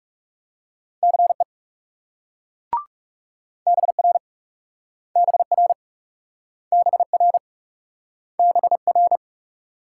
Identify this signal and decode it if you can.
Morse code, heard as a keyed tone, about 38 wpm, 700 Hz. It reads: CE BR BR BR BR